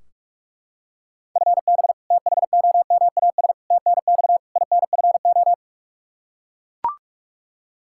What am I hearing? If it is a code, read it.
Morse code, heard as a keyed tone, about 43 wpm, 700 Hz. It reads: FB THOMAS TNX INFO
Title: Practice QSO - Exchange 3 - More Info - 45wpm